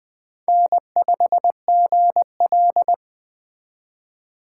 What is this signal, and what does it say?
Morse code, heard as a keyed tone, about 21 wpm, 700 Hz. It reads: N5GL
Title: Speed Racing - USA Call Signs 17wpm